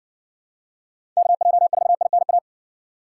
Morse code, heard as a keyed tone, about 59 wpm, 700 Hz. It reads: KJ4INA